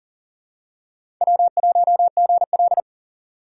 Morse code, heard as a keyed tone, about 38 wpm, 700 Hz. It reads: W1GL
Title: USA Call Signs - 4 Characters - 3x Character Spacing - 40wpm